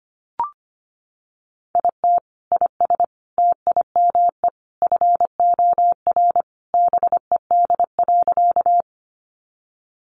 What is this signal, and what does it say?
Morse code, heard as a keyed tone, about 25 wpm, 700 Hz. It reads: IT IS TIME FOR BED.
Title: Sentences from Top 500 Words 25wpm